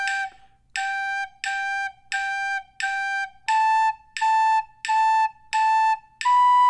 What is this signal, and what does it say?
There is no code.